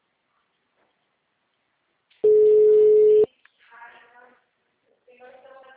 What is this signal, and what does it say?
Ringback tone (Europe, 425 Hz), with speech.